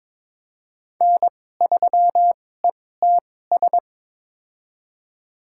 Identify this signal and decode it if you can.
Morse code, heard as a keyed tone, about 22 wpm, spaced as for 16 wpm, 700 Hz. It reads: N3ETS
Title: USA Call Signs - 5 Characters - 2x Character Spacing - 22wpm